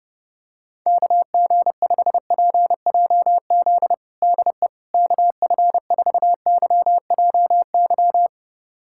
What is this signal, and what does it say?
Morse code, heard as a keyed tone, about 32 wpm, 700 Hz. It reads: KG5PJZ DE KF4YJY
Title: USA Callsign DE Callsign - 6 Characters - 30wpm